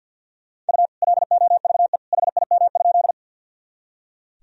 Morse code, heard as a keyed tone, about 46 wpm, 700 Hz. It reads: U LOVE HIM?
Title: Sentences from Top 400 Words and Abbreviations 50wpm